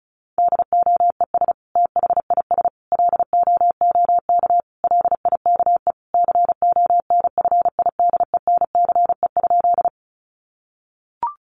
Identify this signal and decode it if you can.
Morse code, heard as a keyed tone, about 34 wpm, 700 Hz. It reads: DOES THIS LOOK LIKE CONFIDENCE?
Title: Sentences from Top 1500 Words 35wpm